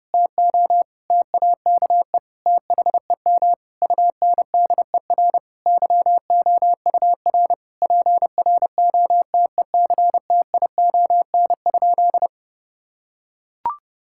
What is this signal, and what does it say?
Morse code, heard as a keyed tone, about 30 wpm, 700 Hz. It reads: TO TAKE THEM UNDER YOUR PROTECTION?
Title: Sentences from Top 1600 Words 30wpm